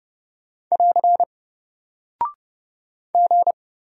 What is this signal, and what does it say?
Morse code, heard as a keyed tone, about 30 wpm, 700 Hz. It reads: <AR> G
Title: Single Character (Letter:Number:Punctuation:Prosign) Alt 30wpm